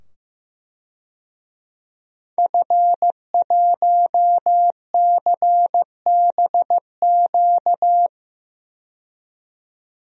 Morse code, heard as a keyed tone, about 16 wpm, 700 Hz. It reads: F1CBQ
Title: DXCC-100 Call Signs - 5 Characters - 15wpm